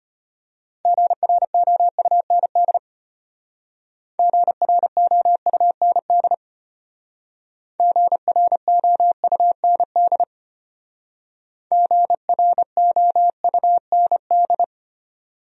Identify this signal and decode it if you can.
Morse code, heard as a keyed tone, about 32 wpm, 700 Hz. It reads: GROUND GROUND GROUND GROUND